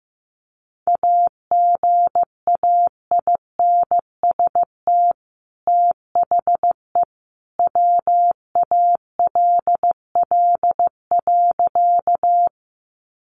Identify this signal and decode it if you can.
Morse code, heard as a keyed tone, about 15 wpm, 700 Hz. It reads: AGAINST THE WALL.
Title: Sentences from Top 1100 Words 15wpm